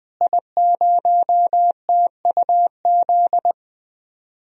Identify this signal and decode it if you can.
Morse code, heard as a keyed tone, about 20 wpm, 700 Hz. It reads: I0TUZ
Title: DXCC-100 Call Signs - 5 Characters - Repeated 3x - 20wpm